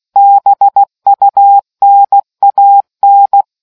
Morse code, heard as a keyed tone, about 16 wpm, 800 Hz. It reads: BUNAN